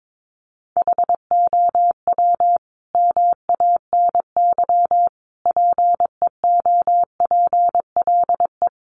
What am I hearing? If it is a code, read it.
Morse code, heard as a keyed tone, about 22 wpm, 700 Hz. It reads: HOW MANY PEOPLE
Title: Sentences from Top 1200 Words 22wpm